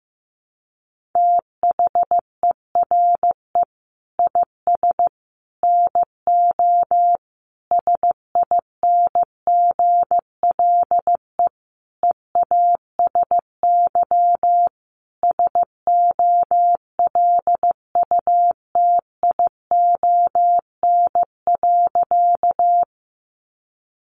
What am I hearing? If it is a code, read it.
Morse code, heard as a keyed tone, about 14 wpm, 700 Hz. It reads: THERE IS NO SINGLE EASY SOLUTION.
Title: Sentences from Top 1300 Words 15wpm